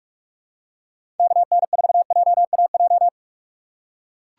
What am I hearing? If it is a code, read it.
Morse code, heard as a keyed tone, about 42 wpm, 700 Hz. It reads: KN4JAJ
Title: USA Call Signs - 6 Characters - 2x Character Spacing - 45wpm